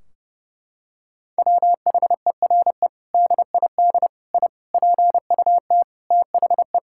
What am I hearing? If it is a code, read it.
Morse code, heard as a keyed tone, about 30 wpm, 700 Hz. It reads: WHERE DID I PUT THE
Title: Sentences from Top 1600 Words 30wpm